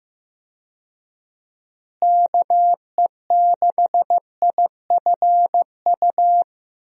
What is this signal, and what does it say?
Morse code, heard as a keyed tone, about 15 wpm, 700 Hz. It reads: KE6IFU
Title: USA Call Signs - 6 Characters - Repeated 2x - 15wpm